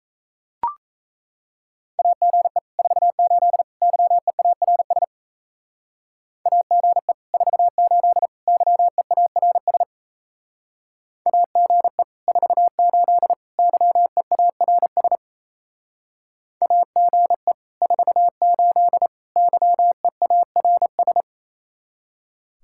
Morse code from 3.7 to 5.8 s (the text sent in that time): YEARS